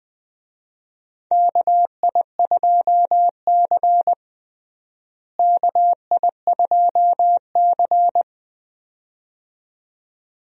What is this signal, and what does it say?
Morse code, heard as a keyed tone, about 20 wpm, 700 Hz. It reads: KI2C KI2C
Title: USA Call Signs - 4 Characters - Repeated 2x - 20wpm